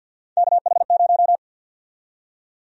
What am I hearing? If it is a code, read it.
Morse code, heard as a keyed tone, about 46 wpm, 700 Hz. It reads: KH0
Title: USA Call Sign Prefixes - 50wpm